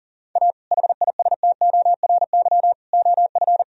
Morse code, heard as a keyed tone, about 40 wpm, 700 Hz. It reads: A HISTORY OF
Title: Sentences from Top 1000 Words - Review Part 3 - 40wpm